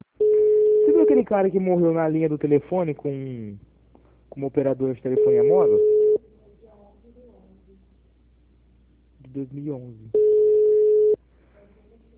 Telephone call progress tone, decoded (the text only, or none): ringback tone